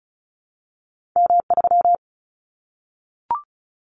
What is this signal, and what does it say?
Morse code, heard as a keyed tone, about 34 wpm, 700 Hz. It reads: M3
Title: Callsign Prefixes for Top 25 DXCC - Repeated 3x - 35wpm